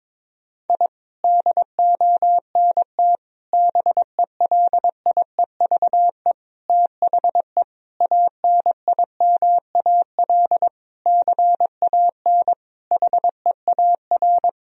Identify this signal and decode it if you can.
Morse code, heard as a keyed tone, about 22 wpm, 700 Hz. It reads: I DONT BELIEVE THE ANIMAL CAN HEAR
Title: Sentences from Top 1000 Words - Review Part 1 - 22wpm